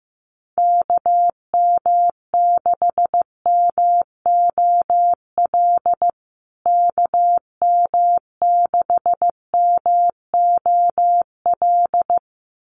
Morse code, heard as a keyed tone, about 15 wpm, 700 Hz. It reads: KM6MOL KM6MOL
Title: USA Call Signs Repeated 2x 15wpm